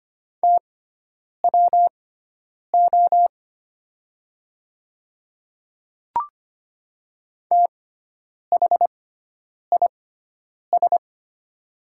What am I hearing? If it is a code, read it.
Morse code, heard as a keyed tone, about 25 wpm, spaced as for 9 wpm, 700 Hz. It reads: TWO THIS
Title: 6x Character Spacing - Top 100 Words - 25wpm